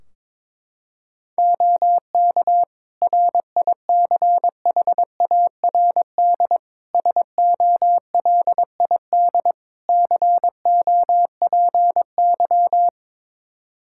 Morse code, heard as a keyed tone, about 22 wpm, 700 Hz. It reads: OK RICHARD SOLID COPY
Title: Practice QSO - Exchange 3 - More Info - 22wpm